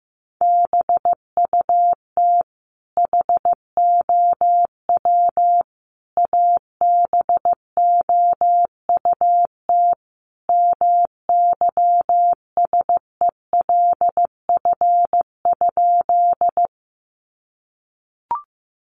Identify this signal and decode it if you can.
Morse code, heard as a keyed tone, about 15 wpm, 700 Hz. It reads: BUT HOW ABOUT MYSELF?
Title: Sentences from Top 1300 Words 15wpm